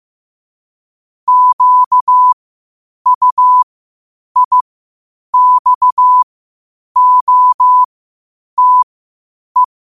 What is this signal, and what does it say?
Morse code, heard as a keyed tone, about 15 wpm, spaced as for 8 wpm, 1000 Hz. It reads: QUIXOTE
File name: qotdfiles / 8wpm